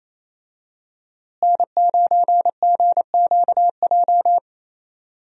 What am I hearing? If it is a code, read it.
Morse code, heard as a keyed tone, about 28 wpm, 700 Hz. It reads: N9GQJ
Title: USA Call Signs - 5 Characters - Repeated 3x - 28wpm